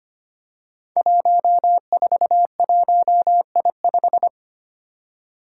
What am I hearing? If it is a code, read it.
Morse code, heard as a keyed tone, about 25 wpm, 700 Hz. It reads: 141I5
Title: Five Challenging Letter-Number Combinations 25wpm